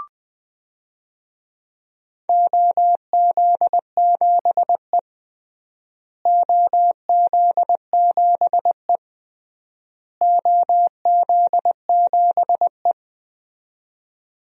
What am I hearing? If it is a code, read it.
Morse code, heard as a keyed tone, about 20 wpm, 700 Hz. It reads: OZ7E OZ7E OZ7E
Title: DXCC-100 Call Signs - 4 Characters - Repeated 3x - 20wpm